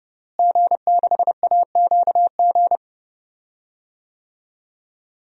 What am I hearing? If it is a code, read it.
Morse code, heard as a keyed tone, about 28 wpm, 700 Hz. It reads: G6AQG
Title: DXCC-100 Call Signs - 5 Characters - Repeated 3x - 30wpm